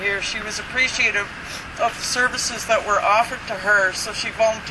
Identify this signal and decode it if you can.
Emergency Alert System (SAME) tones, heard as harsh buzzing bursts, with speech. It is heard from 0.6 to 4.0 s.